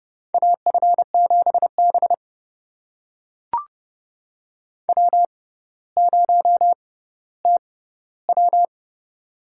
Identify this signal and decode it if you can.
Morse code, heard as a keyed tone, about 30 wpm, spaced as for 15 wpm, 700 Hz. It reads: AF7B W0TW
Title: USA Call Signs - 6x Character Spacing - 30wpm